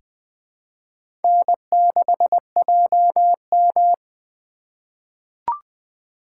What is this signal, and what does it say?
Morse code, heard as a keyed tone, about 20 wpm, 700 Hz. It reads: N6JM
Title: USA Call Signs - 2x Character Spacing - 20wpm